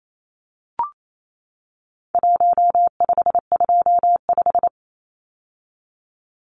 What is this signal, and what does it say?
Morse code, heard as a keyed tone, about 28 wpm, 700 Hz. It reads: 1525
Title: Numbers - Contest Style - 4-Digit 28wpm